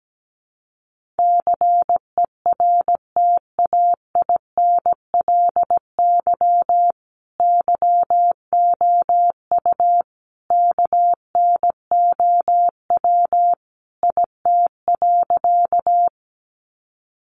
Morse code, heard as a keyed tone, about 17 wpm, 700 Hz. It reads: CERTAINLY YOU KNOW IT.
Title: Sentences from Top 900 Words 17wpm